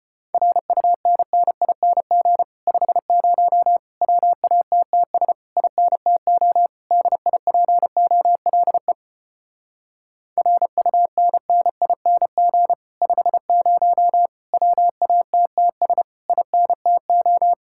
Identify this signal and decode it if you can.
Morse code, heard as a keyed tone, about 33 wpm, 700 Hz. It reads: RUNNING 50 WATTS INTO DIPOLE RUNNING 50 WATTS INTO